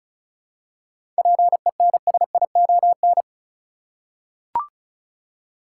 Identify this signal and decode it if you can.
Morse code, heard as a keyed tone, about 36 wpm, 700 Hz. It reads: PENSION
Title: Seven Letter Words - 35wpm